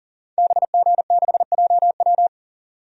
Morse code, heard as a keyed tone, about 40 wpm, 700 Hz. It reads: BG6JW
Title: DXCC-100 Call Signs - 5 Characters - 40wpm